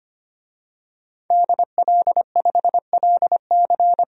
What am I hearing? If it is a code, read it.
Morse code, heard as a keyed tone, about 25 wpm, 700 Hz. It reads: DL5LC
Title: DXCC-100 Call Signs - 5 Characters - 25wpm